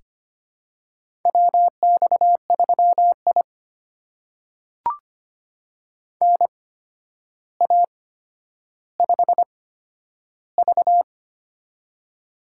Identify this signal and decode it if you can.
Morse code, heard as a keyed tone, about 25 wpm, spaced as for 11 wpm, 700 Hz. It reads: WX3I NA5V
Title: USA Call Signs - 4 Characters - 8x Character Spacing - 25wpm